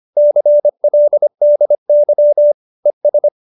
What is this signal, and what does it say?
Morse code, heard as a keyed tone, about 25 wpm, 590 Hz. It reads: CLDY ES